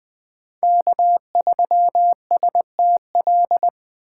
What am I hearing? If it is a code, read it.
Morse code, heard as a keyed tone, about 20 wpm, 700 Hz. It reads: K3STL
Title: 2023 POTA Hunter Call Signs - 20wpm